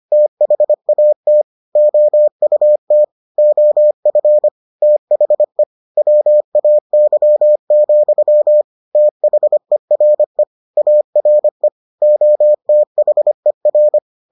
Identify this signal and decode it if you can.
Morse code, heard as a keyed tone, about 25 wpm, 600 Hz. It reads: THAT OUT OF THE WAY, THERE ARE OTHER